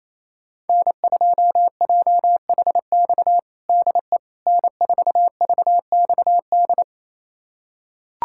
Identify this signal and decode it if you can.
Morse code, heard as a keyed tone, about 27 wpm, 700 Hz. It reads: N2JHX DE N4VXD
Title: USA Callsign DE Callsign - 5 Characters - 28wpm